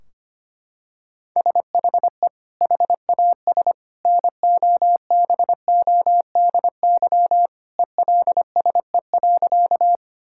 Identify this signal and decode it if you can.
Morse code, heard as a keyed tone, about 25 wpm, 700 Hz. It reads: SHE HAS NOBODY ELSE.
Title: Sentences from Top 1600 Words 25wpm